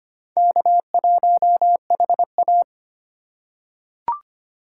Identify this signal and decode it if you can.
Morse code, heard as a keyed tone, about 25 wpm, 700 Hz. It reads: K1HA
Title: USA Call Signs - 4 Characters - 4x Character Spacing - 25wpm